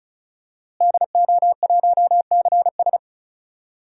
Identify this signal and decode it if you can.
Morse code, heard as a keyed tone, about 36 wpm, 700 Hz. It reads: DO1CS
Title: DXCC-100 Call Signs - 5 Characters - Repeated 3x - 35wpm